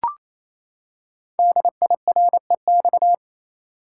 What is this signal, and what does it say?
Morse code, heard as a keyed tone, about 27 wpm, 700 Hz. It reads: DIREX
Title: Code Groups - 5 Letters - 28wpm